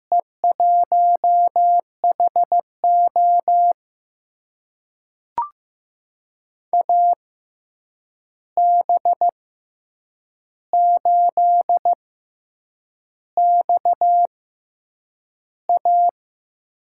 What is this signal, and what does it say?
Morse code, heard as a keyed tone, about 15 wpm, spaced as for 7 wpm, 700 Hz. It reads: E1HO AB8XA
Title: USA Call Signs - 5 Characters - 6x Character Spacing - 15wpm